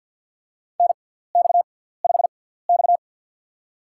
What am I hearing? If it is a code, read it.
Morse code, heard as a keyed tone, about 53 wpm, spaced as for 17 wpm, 690 Hz. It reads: NX5X
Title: USA Call Signs - 4 Characters - 6x Character Spacing - 50wpm